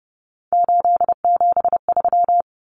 Morse code, heard as a keyed tone, about 30 wpm, 700 Hz. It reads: 873
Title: Common Cut Numbers - 3-Digit 30wpm